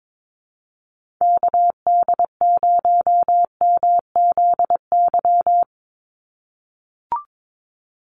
Morse code, heard as a keyed tone, about 22 wpm, 700 Hz. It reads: KD0MZY
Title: USA Call Signs 22wpm